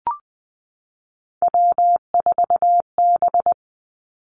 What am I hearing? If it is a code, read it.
Morse code, heard as a keyed tone, about 20 wpm, 700 Hz. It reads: W4B
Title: Three Challenging Letter-Number Combinations 20wpm